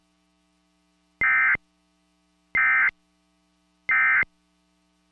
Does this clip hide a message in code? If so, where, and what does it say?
Yes: Emergency Alert System (SAME) tones from 1.2 to 4.2 s, heard as harsh buzzing bursts.